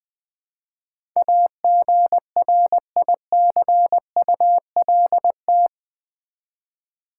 Morse code, heard as a keyed tone, about 20 wpm, 700 Hz. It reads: AGRICULT
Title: Build A Word - 4 Letters At A Time - 11-Letter Words - 20wpm